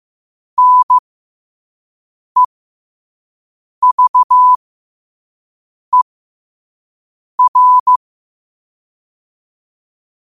Morse code, heard as a keyed tone, about 15 wpm, spaced as for 5 wpm, 1000 Hz. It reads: NEVER
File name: qotdfiles / 5wpm